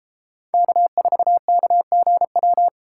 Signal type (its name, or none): Morse code